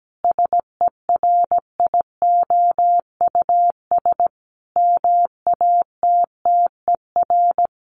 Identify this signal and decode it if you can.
Morse code, heard as a keyed tone, about 17 wpm, 700 Hz. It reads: SERIOUS MATTER
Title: Sentences from Top 1100 Words 17wpm